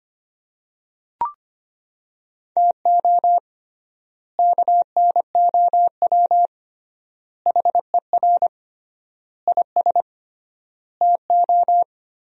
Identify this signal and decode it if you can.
Morse code, heard as a keyed tone, about 25 wpm, 700 Hz. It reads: TO KNOW HER IS TO
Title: 3x Word-Spacing - Sentences from Top 400 Words - 25wpm